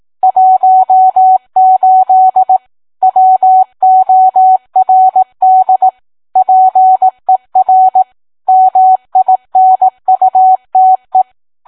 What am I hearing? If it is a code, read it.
Morse code, heard as a keyed tone, about 18 wpm, 750 Hz. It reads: 18 WORD PER MINUTE